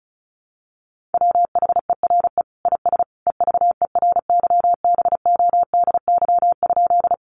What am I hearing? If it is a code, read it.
Morse code, heard as a keyed tone, about 34 wpm, 700 Hz. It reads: WHERE IS EVERYBODY?
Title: Sentences from Top 1400 Words 35wpm